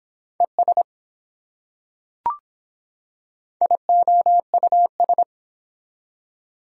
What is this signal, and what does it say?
Morse code, heard as a keyed tone, about 24 wpm, 700 Hz. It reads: ES IOUS